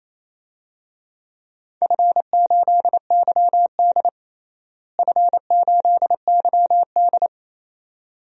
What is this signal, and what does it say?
Morse code, heard as a keyed tone, about 28 wpm, 700 Hz. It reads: F8YD F8YD
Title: DXCC-100 Call Signs - 4 Characters - Repeated 2x - 28wpm